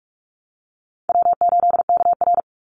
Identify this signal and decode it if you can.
Morse code, heard as a keyed tone, about 46 wpm, 700 Hz. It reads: W8KR
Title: Easy Call Signs - 45wpm